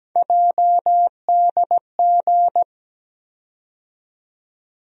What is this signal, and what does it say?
Morse code, heard as a keyed tone, about 17 wpm, 700 Hz. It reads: JDG
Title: USA Call Sign Postfixes - 3 Letters 17wpm